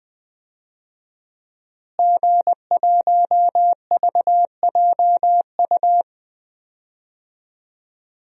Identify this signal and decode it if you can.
Morse code, heard as a keyed tone, about 20 wpm, 700 Hz. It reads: G1VJU
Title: DXCC-100 Call Signs - 5 Characters - Repeated 3x - 20wpm